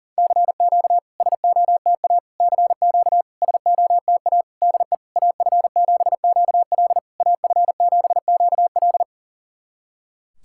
Morse code, heard as a keyed tone, about 37 wpm, 700 Hz. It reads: CQ SOTA CQ SOTA DE AF7QL AF7QL
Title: SOTA QSO Practice 40wpm